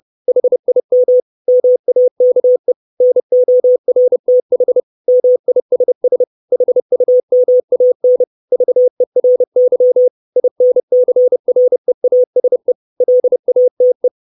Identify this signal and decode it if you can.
Morse code, heard as a keyed tone, about 30 wpm, 500 Hz. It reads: HIM MAKE NORTH MISS HUMAN VERY INCREASE LATE
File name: 632 Words 30WPM